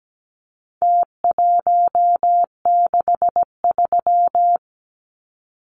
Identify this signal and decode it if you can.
Morse code, heard as a keyed tone, about 17 wpm, 700 Hz. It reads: T163
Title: Numbers - Contest Style - 4-Digit 17wpm